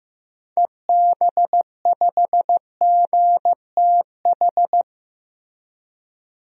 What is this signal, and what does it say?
Morse code, heard as a keyed tone, about 15 wpm, 700 Hz. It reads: EB5GTH
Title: DXCC-100 Call Signs - 6 Characters - Repeated 3x - 15wpm